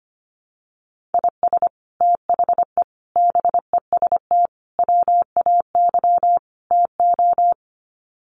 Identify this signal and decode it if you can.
Morse code, heard as a keyed tone, about 25 wpm, 700 Hz. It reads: IS THE BEST WAY TO
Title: Sets of 5 Words Encore 25wpm